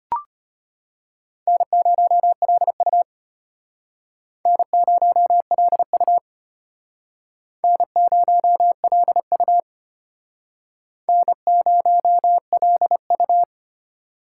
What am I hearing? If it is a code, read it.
Morse code, heard as a keyed tone, about 34 wpm, 700 Hz. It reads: N0LU N0LU N0LU N0LU